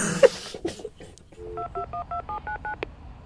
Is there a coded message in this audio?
Yes: DTMF from 1.6 to 2.7 s, heard as telephone touch tones, tones of about 110 ms. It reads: 2212799